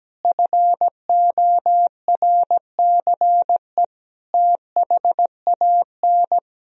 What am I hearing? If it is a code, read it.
Morse code, heard as a keyed tone, about 17 wpm, 700 Hz. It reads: FORCE THAN
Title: Sentences from Top 1000 Words - Review Part 3 - 17wpm